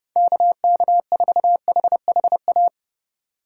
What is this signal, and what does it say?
Morse code, heard as a keyed tone, about 28 wpm, 700 Hz. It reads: KK4HHA